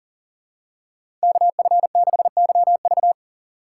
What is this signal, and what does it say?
Morse code, heard as a keyed tone, about 37 wpm, 700 Hz. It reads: KF6YV
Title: USA Call Signs - 5 Characters - 6x Character Spacing - 40wpm